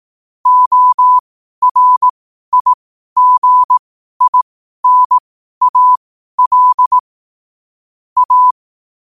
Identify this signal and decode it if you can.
Morse code, heard as a keyed tone, about 18 wpm, spaced as for 12 wpm, 1000 Hz. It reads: ORIGINAL A
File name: 12wpm